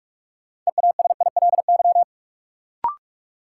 Morse code, heard as a keyed tone, about 44 wpm, 700 Hz. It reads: EASILY
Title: Sentences from Top 900 Words - New Word Drill - 45wpm